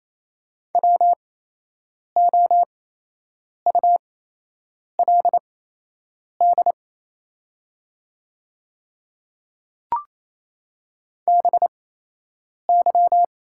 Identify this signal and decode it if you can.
Morse code, heard as a keyed tone, about 28 wpm, spaced as for 8 wpm, 690 Hz. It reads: WOULD BY
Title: 8x Character Spacing - Top 100 Words - 28wpm